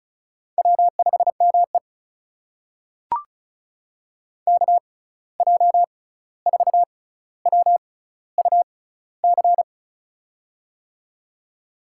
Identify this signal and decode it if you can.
Morse code, heard as a keyed tone, about 36 wpm, spaced as for 16 wpm, 700 Hz. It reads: W5ME KJ4WUC
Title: USA Call Signs - 6x Character Spacing - 35wpm